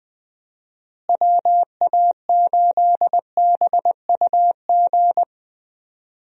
Morse code, heard as a keyed tone, about 20 wpm, 700 Hz. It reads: WA8BUG
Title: USA Call Signs - 6 Characters - Repeated 2x - 20wpm